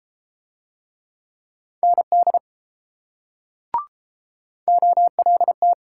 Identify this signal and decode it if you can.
Morse code, heard as a keyed tone, about 32 wpm, 700 Hz. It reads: ND OLT